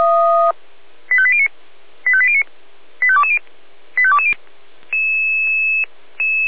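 Five-tone selective call (ZVEI), heard as quick runs of five tones, tones of about 70 ms. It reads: 85809 85809 85220 85220
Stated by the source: Five-tone ZVEI selective call received by SDR.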